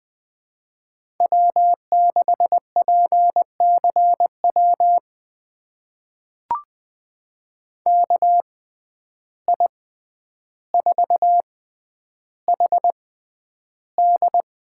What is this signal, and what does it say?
Morse code, heard as a keyed tone, about 20 wpm, spaced as for 10 wpm, 700 Hz. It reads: W6PCW KI4HD
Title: USA Call Signs - 6x Character Spacing - 20wpm